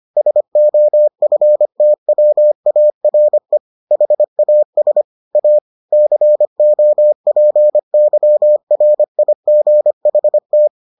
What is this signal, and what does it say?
Morse code, heard as a keyed tone, about 25 wpm, 600 Hz. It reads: SOFTWARE HAS A COPYRIGHT